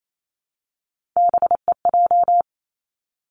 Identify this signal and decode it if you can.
Morse code, heard as a keyed tone, about 28 wpm, 700 Hz. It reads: BEJ
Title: Three Challenging Letter-Number Combinations 28wpm